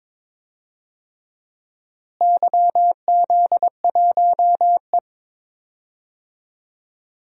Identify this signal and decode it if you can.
Morse code, heard as a keyed tone, about 21 wpm, 700 Hz. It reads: YZ1E
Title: DXCC-100 Call Signs - 4 Characters - Repeated 3x - 22wpm